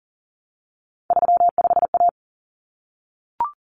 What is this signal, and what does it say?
Morse code, heard as a keyed tone, about 40 wpm, 700 Hz. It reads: 35A
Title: Common Cut Numbers - 3-Digit 40wpm